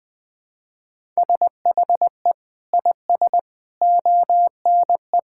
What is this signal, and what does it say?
Morse code, heard as a keyed tone, about 21 wpm, 700 Hz. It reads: SHE IS ONE
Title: Sentences with 5-Word Phrases - 20wpm